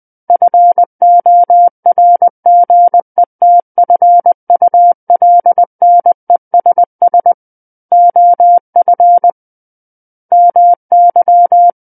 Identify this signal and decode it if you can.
Morse code, heard as a keyed tone, about 20 wpm, 700 Hz. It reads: FORGETFULNESS OF MY
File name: 20 WPM